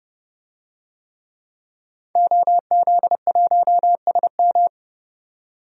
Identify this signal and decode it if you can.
Morse code, heard as a keyed tone, about 32 wpm, 700 Hz. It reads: OZ1SM
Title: DXCC-100 Call Signs - 5 Characters - 30wpm